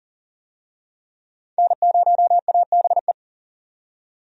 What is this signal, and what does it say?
Morse code, heard as a keyed tone, about 40 wpm, 700 Hz. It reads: N0ABE
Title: USA Call Signs - 5 Characters - 40wpm